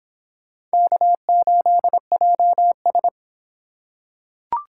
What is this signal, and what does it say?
Morse code, heard as a keyed tone, about 26 wpm, 700 Hz. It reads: K8JS